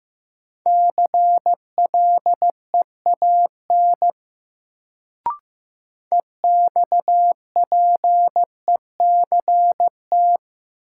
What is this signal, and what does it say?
Morse code, heard as a keyed tone, about 15 wpm, 700 Hz. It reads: CLEAN EXPECT
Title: Sentences from Top 1100 Words - New Word Drill Alt - 15wpm